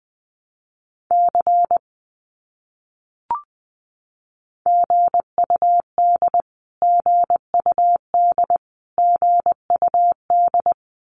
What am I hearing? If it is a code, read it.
Morse code, heard as a keyed tone, about 20 wpm, 700 Hz. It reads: C GUD GUD GUD
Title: Abbreviations Repeated 3x - 20wpm